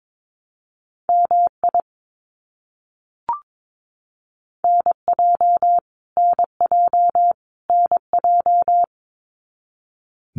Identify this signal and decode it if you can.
Morse code, heard as a keyed tone, about 22 wpm, 700 Hz. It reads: MI NJ NJ NJ
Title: US States Repeated 3x 22wpm